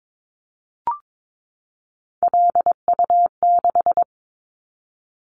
Morse code, heard as a keyed tone, about 22 wpm, 700 Hz. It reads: LU6
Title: Callsign Prefixes for Top 100 DXCC - 22wpm